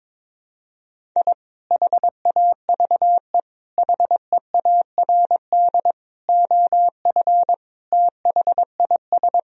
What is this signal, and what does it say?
Morse code, heard as a keyed tone, about 22 wpm, 700 Hz. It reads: I HAVE HEARD OF THIS
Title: Sentences from Top 1000 Words - Review Part 1 - 22wpm